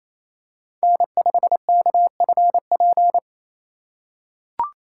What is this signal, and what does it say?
Morse code, heard as a keyed tone, about 28 wpm, 700 Hz. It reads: N5KFP
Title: USA Call Signs - 5 Characters - 2x Character Spacing - 28wpm